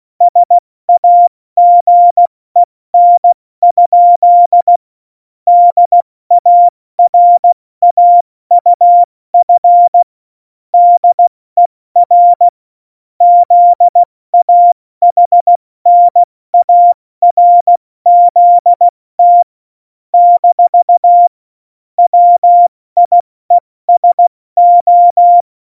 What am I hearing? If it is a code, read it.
Morse code, heard as a keyed tone, about 16 wpm, 700 Hz. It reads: SAGEN? DARAUF DER ZAHNARZT - WIESO
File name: Morsekurs / Witze / Witz 05. Dezember 2017, 70 F80